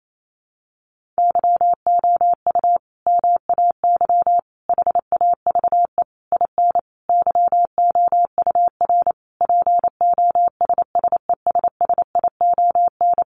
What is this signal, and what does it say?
Morse code, heard as a keyed tone, about 28 wpm, 700 Hz. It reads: YOU MAY HAVE IN YOUR POSSESSION
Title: Sentences from Top 1400 Words 28wpm